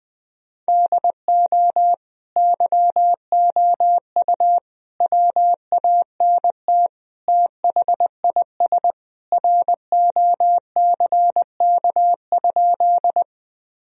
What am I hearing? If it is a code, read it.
Morse code, heard as a keyed tone, about 20 wpm, 700 Hz. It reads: DO YOU WANT THIS ROCK?
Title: Sentences from Top 1000 Words - Review Part 2 - 20wpm